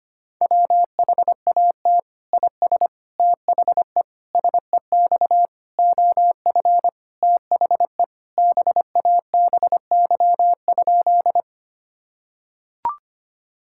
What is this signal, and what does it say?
Morse code, heard as a keyed tone, about 25 wpm, 700 Hz. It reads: WHAT IS THE SEX OF THE BABY?
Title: Sentences from Top 1600 Words 25wpm